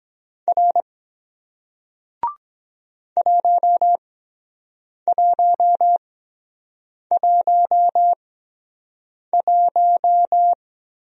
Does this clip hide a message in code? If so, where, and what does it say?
Yes: Morse code from 0.5 to 10.5 s, heard as a keyed tone, about 23 wpm, spaced as for 7 wpm, 700 Hz. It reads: R 1111